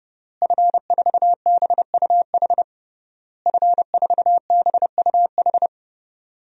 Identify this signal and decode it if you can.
Morse code, heard as a keyed tone, about 31 wpm, 700 Hz. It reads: F4BUH F4BUH
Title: DXCC-100 Call Signs - 5 Characters - Repeated 3x - 30wpm